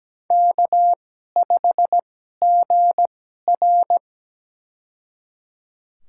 Morse code, heard as a keyed tone, about 17 wpm, spaced as for 12 wpm, 700 Hz. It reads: K5GR
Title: USA Call Signs - 4 Characters - 2x Character Spacing - 17wpm